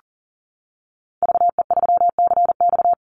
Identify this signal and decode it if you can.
Morse code, heard as a keyed tone, about 38 wpm, 700 Hz. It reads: VE3CX
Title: Call Signs - 40wpm